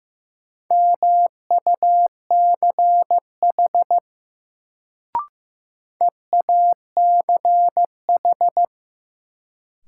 Morse code, heard as a keyed tone, about 15 wpm, 700 Hz. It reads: MUCH EACH
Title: Sentences from Top 200 Words - New Word Drill - 15wpm